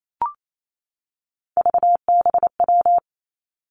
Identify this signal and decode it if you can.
Morse code, heard as a keyed tone, about 28 wpm, 700 Hz. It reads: VBW
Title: Three Challenging Letter-Number Combinations 28wpm